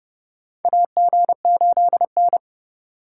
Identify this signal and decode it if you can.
Morse code, heard as a keyed tone, about 30 wpm, 700 Hz. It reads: AG8N
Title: USA Call Signs - 2x Character Spacing - 30wpm